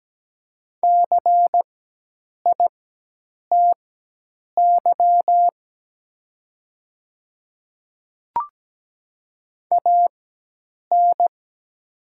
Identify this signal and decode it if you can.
Morse code, heard as a keyed tone, about 17 wpm, spaced as for 8 wpm, 700 Hz. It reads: CITY AN
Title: 4x Character Spacing - Top 300 Words - 17wpm